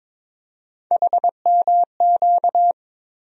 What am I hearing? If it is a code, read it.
Morse code, heard as a keyed tone, about 23 wpm, 700 Hz. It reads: HMQ
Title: USA Call Sign Postfixes - 3 Letters 22wpm